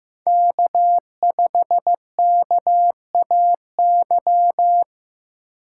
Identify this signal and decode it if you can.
Morse code, heard as a keyed tone, about 15 wpm, 700 Hz. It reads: K5KAY
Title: USA Call Signs - 5 Characters - 4x Character Spacing - 15wpm